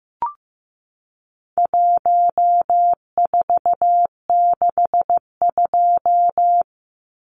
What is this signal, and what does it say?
Morse code, heard as a keyed tone, about 15 wpm, 700 Hz. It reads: 1462
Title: Numbers - 4-Digit 15wpm